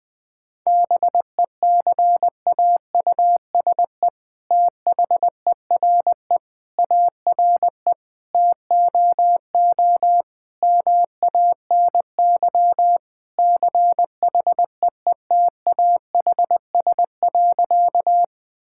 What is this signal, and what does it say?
Morse code, heard as a keyed tone, about 20 wpm, 700 Hz. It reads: BECAUSE THERE ARE TOO MANY CHEETAHS.
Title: All-Age Appropriate Jokes - 20wpm